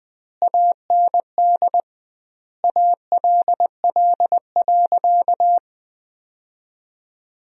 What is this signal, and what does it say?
Morse code, heard as a keyed tone, about 20 wpm, 700 Hz. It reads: AND ALL.
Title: 2x Word-Spacing - Sentences from Top 200 Words - 20wpm